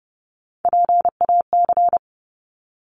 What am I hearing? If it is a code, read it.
Morse code, heard as a keyed tone, about 30 wpm, 700 Hz. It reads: PAC
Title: ARRL Sections 30wpm